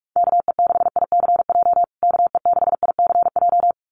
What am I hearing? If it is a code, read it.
Morse code, heard as a keyed tone, about 44 wpm, 700 Hz. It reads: KE6ICJ KE6ICJ
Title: USA Call Signs Repeated 2x 45wpm